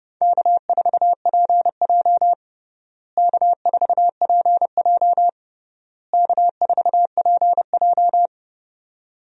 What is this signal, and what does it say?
Morse code, heard as a keyed tone, about 32 wpm, 700 Hz. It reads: K4PJ K4PJ K4PJ
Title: USA Call Signs - 4 Characters - Repeated 3x - 30wpm